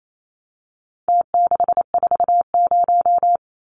Morse code, heard as a keyed tone, about 28 wpm, 700 Hz. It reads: T640
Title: Numbers - Contest Style - 4-Digit 28wpm